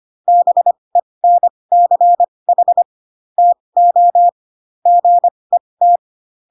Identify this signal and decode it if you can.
Morse code, heard as a keyed tone, about 25 wpm, spaced as for 20 wpm, 700 Hz. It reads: BENCH TO GET
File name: Pinocchio / Pinocchio ch2 15 25 ik1hge